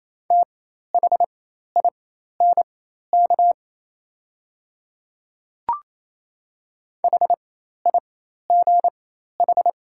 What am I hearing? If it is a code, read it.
Morse code, heard as a keyed tone, about 28 wpm, spaced as for 13 wpm, 700 Hz. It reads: THINK HIGH
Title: 4x Character Spacing - Top 200 Words - 28wpm